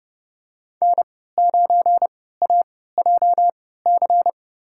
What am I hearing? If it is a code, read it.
Morse code, heard as a keyed tone, about 30 wpm, spaced as for 17 wpm, 700 Hz. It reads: N9AJC
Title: USA Call Signs - 5 Characters - 3x Character Spacing - 30wpm